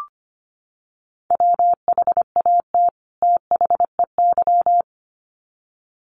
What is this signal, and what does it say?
Morse code, heard as a keyed tone, about 25 wpm, 700 Hz. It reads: WHAT THEY
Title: Sets of 2 Words 25wpm